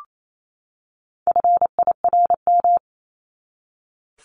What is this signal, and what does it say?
Morse code, heard as a keyed tone, about 28 wpm, 700 Hz. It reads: FIRM
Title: Top 1251-1500 Words 28wpm